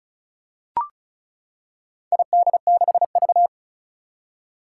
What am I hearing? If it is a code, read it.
Morse code, heard as a keyed tone, about 36 wpm, 700 Hz. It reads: ID6V
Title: Four Challenging Letter-Number Combinations 35wpm